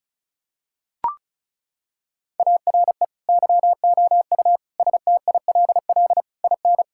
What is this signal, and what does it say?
Morse code, heard as a keyed tone, about 34 wpm, 700 Hz. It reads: ARE YOU STILL IN
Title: Sentences from Top 1600 Words 35wpm